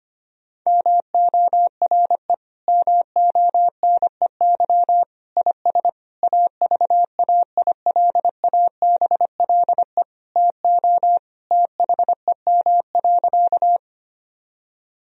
Morse code, heard as a keyed tone, about 25 wpm, 700 Hz. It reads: MORE MONEY IS AVAILABLE TO THEM.
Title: Sentences from Top 1000 Words - Review Part 3 - 25wpm